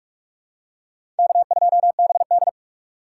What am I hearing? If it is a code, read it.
Morse code, heard as a keyed tone, about 46 wpm, 700 Hz. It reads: K2BD
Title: USA Call Signs - 2x Character Spacing - 45wpm